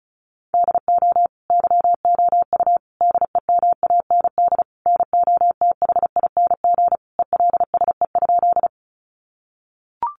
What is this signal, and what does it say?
Morse code, heard as a keyed tone, about 34 wpm, 700 Hz. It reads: DO YOU DEMAND NOTHING ELSE?
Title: Sentences from Top 800 Words 35wpm